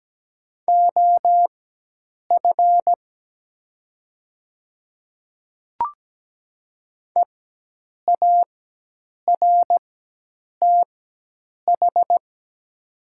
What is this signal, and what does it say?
Morse code, heard as a keyed tone, about 17 wpm, spaced as for 8 wpm, 700 Hz. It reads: OF EARTH
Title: 4x Character Spacing - Top 200 Words - 17wpm